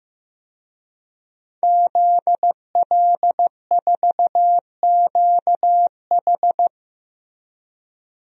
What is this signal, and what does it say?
Morse code, heard as a keyed tone, about 15 wpm, 700 Hz. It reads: ZL4QH
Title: DXCC-100 Call Signs - 5 Characters - 15wpm